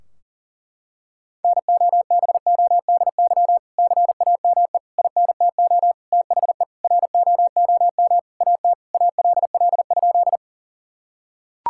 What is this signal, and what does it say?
Morse code, heard as a keyed tone, about 40 wpm, 700 Hz. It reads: NOBODY CAME INTO THE ROOM AT ALL?
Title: Sentences from Top 1600 Words 40wpm